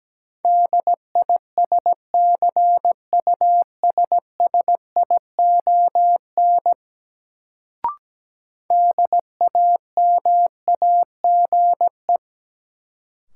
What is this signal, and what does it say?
Morse code, heard as a keyed tone, about 17 wpm, 700 Hz. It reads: DISCUSSION DAMAGE
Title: Sentences from Top 1100 Words - New Word Drill Alt - 17wpm